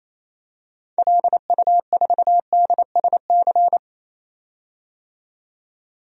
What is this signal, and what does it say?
Morse code, heard as a keyed tone, about 28 wpm, 700 Hz. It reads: LU4DSC
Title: DXCC-100 Call Signs - 6 Characters - 28wpm